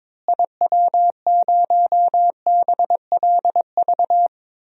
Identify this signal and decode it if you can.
Morse code, heard as a keyed tone, about 21 wpm, 700 Hz. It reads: IW0BLV
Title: DXCC-100 Call Signs - 6 Characters - Repeated 2x - 22wpm